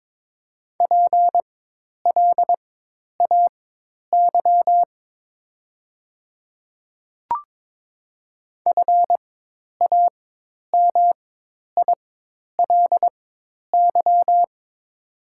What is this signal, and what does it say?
Morse code, heard as a keyed tone, about 22 wpm, spaced as for 10 wpm, 700 Hz. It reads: PLAY FAMILY
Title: 4x Character Spacing - Top 500 Words - 22wpm